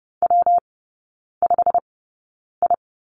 Morse code, heard as a keyed tone, about 30 wpm, 700 Hz. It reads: W 5 I
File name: Lesson 19e - Letter w - Three Characters 30wpm